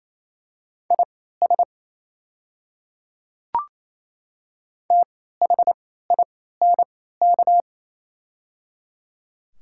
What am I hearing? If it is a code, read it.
Morse code, heard as a keyed tone, about 27 wpm, spaced as for 16 wpm, 700 Hz. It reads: IS THINK
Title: 3x Character Spacing - Top 200 Words - 28wpm